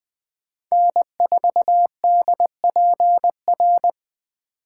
Morse code, heard as a keyed tone, about 20 wpm, 700 Hz. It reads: N4DPR